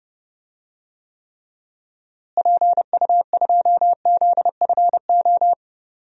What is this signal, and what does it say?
Morse code, heard as a keyed tone, about 31 wpm, 700 Hz. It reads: PU2ZFO
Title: DXCC-100 Call Signs - 6 Characters - Repeated 2x - 30wpm